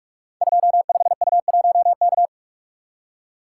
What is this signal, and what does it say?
Morse code, heard as a keyed tone, about 44 wpm, 700 Hz. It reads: 25U1K
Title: Code Groups - 5 Letter:Numbers - 45wpm